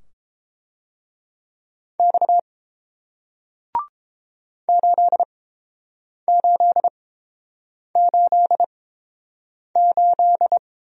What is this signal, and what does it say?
Morse code, heard as a keyed tone, about 30 wpm, spaced as for 7 wpm, 700 Hz. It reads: X 8888